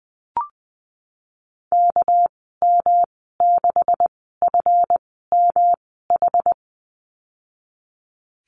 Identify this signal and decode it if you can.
Morse code, heard as a keyed tone, about 20 wpm, spaced as for 14 wpm, 700 Hz. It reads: KM6FMH
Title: 213 Lesson 34g - Number 8 - Call Signs 20wpm